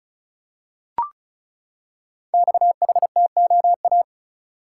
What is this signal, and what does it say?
Morse code, heard as a keyed tone, about 34 wpm, 700 Hz. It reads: XHTOA
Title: Code Groups - 5 Letters - 35wpm